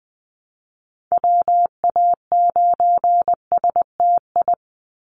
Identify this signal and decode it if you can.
Morse code, heard as a keyed tone, about 20 wpm, 700 Hz. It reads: WA9STI
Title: NA SOTA Activator Callsigns 20wpm